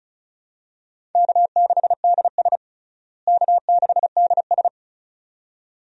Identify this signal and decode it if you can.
Morse code, heard as a keyed tone, about 36 wpm, 700 Hz. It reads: K6DS K6DS
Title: USA Call Signs - 4 Characters - Repeated 2x - 35wpm